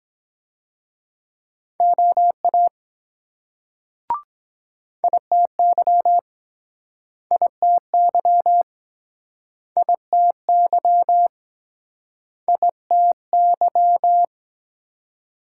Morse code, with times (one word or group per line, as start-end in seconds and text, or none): 1.8-2.7 OA
5.0-6.2 ITY
7.3-8.6 ITY
9.8-11.3 ITY
12.5-14.2 ITY